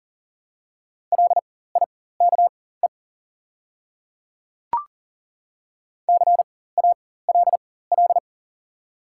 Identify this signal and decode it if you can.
Morse code, heard as a keyed tone, about 40 wpm, spaced as for 19 wpm, 700 Hz. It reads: LIKE CALL
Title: 4x Character Spacing - Top 200 Words - 40wpm